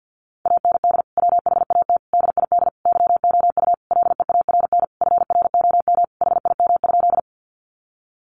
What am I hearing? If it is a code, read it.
Morse code, heard as a keyed tone, about 50 wpm, 700 Hz. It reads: AND WHAT DID YOU LEARN FROM HIM?
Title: Sentences from Top 300 Words 50wpm